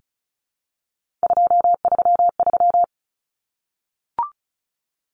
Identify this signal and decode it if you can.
Morse code, heard as a keyed tone, about 34 wpm, 700 Hz. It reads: 233
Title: Numbers - 3-Digit 35wpm